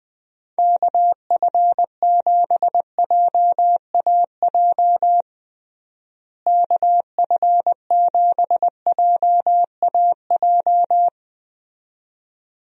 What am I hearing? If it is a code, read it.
Morse code, heard as a keyed tone, about 20 wpm, 700 Hz. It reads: KF7JAJ KF7JAJ
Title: USA Call Signs - 6 Characters - Repeated 2x - 20wpm